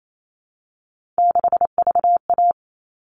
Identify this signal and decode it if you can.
Morse code, heard as a keyed tone, about 28 wpm, 700 Hz. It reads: 6VA
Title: All Cut Numbers - 3-Digit 28wpm